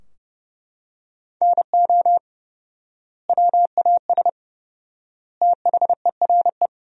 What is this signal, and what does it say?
Morse code, heard as a keyed tone, about 32 wpm, 700 Hz. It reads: NO WAS THERE
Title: 4x Word-Spacing - Sentences from Top 100 Words - 30wpm